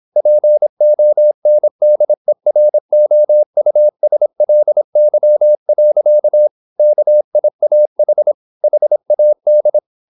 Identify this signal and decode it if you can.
Morse code, heard as a keyed tone, about 26 wpm, 600 Hz. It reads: PONDEROUSLY. KIAH HAD
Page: Never 26wpm | CW med Gnister